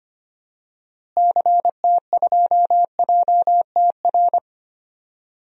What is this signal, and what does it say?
Morse code, heard as a keyed tone, about 25 wpm, 700 Hz. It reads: CT2JTR
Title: DXCC-100 Call Signs - 6 Characters - Repeated 3x - 25wpm